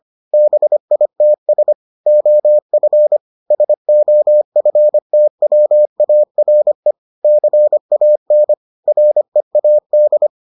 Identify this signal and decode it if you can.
Morse code, heard as a keyed tone, about 25 wpm, 600 Hz. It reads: BITS OF SOFTWARE CAN READ